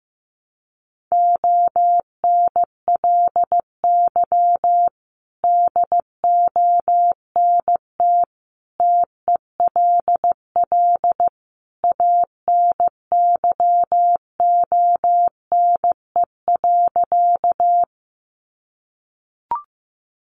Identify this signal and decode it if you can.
Morse code, heard as a keyed tone, about 15 wpm, 700 Hz. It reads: ONLY DONT TELL ANYONE.
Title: Sentences from Top 900 Words 15wpm